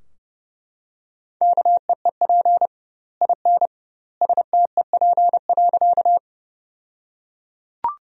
Morse code, heard as a keyed tone, about 30 wpm, 700 Hz. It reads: KEEP IN STEP.
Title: 2x Word-Spacing - Sentences from Top 400 Words - 30wpm